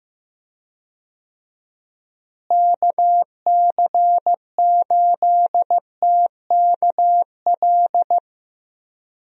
Morse code, heard as a keyed tone, about 16 wpm, 700 Hz. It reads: KC8TKL
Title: USA Call Signs - 6 Characters - Repeated 3x - 15wpm